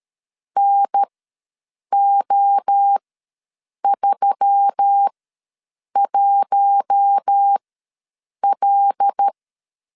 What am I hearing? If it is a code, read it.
Morse code, heard as a keyed tone, about 13 wpm, spaced as for 7 wpm, 800 Hz. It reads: NO31L
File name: LEZIONI / 5 LBVC / 5 LBV ws 12wpm 3015